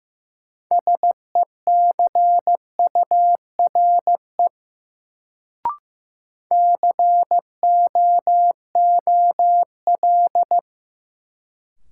Morse code, heard as a keyed tone, about 16 wpm, 700 Hz. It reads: SECURE COOL
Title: Sentences from Top 1200 Words - New Word Drill Alt - 15wpm